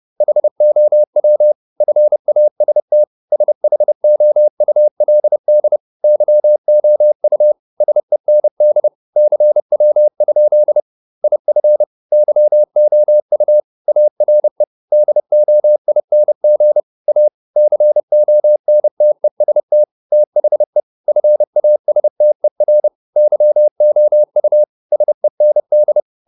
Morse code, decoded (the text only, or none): HOW FAST SHOULD YOU SEND CW? IF YOU ARE DOING A CONTEST THE FASTER YOU SEND